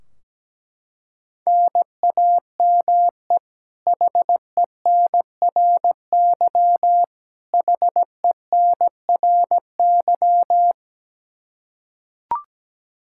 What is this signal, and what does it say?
Morse code, heard as a keyed tone, about 17 wpm, 700 Hz. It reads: NAME HENRY HENRY
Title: Practice QSO - Exchange 1 - Beginning - 17wpm